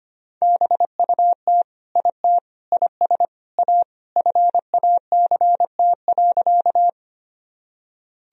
Morse code, heard as a keyed tone, about 25 wpm, 700 Hz. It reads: BUT IT IS A FACT.
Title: Sentences from Top 1000 Words - Review Part 3 - 25wpm